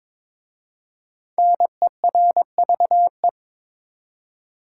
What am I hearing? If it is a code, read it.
Morse code, heard as a keyed tone, about 22 wpm, 700 Hz. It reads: NERVE
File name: 5 Letter Words - First 3 Letters Given - 22wpm